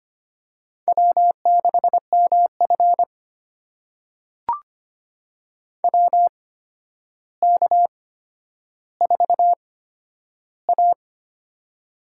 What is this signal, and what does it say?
Morse code, heard as a keyed tone, about 25 wpm, spaced as for 11 wpm, 700 Hz. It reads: W6MF WK4A
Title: USA Call Signs - 4 Characters - 8x Character Spacing - 25wpm